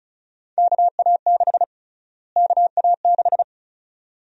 Morse code, heard as a keyed tone, about 34 wpm, 700 Hz. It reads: KA6 KA6
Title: USA Call Sign Prefixes - Repeated 2x - 35wpm